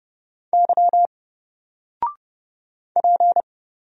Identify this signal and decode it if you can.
Morse code, heard as a keyed tone, about 30 wpm, 700 Hz. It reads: Y P
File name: Single-Letter-Number-30wpm